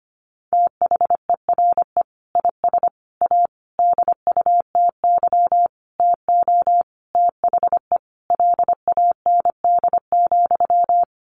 Morse code, heard as a keyed tone, about 25 wpm, 700 Hz. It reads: THERE IS A DUTY TO THE LAND,
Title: Sentences from Top 1500 Words 25wpm